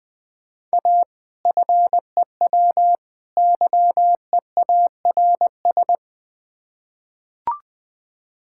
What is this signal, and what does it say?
Morse code, heard as a keyed tone, about 20 wpm, 700 Hz. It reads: A FEW YEARS
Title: Build A Phrase - 3-Word Phrases - 20wpm